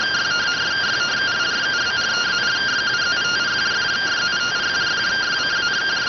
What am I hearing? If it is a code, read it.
Two-tone teleprinter signal, about 170 Hz shift.